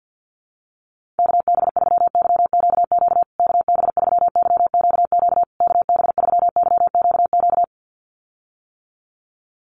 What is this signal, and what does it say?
Morse code, heard as a keyed tone, about 50 wpm, 700 Hz. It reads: KB3YQQ KB3YQQ KB3YQQ
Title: USA Call Signs Repeated 3x 50wpm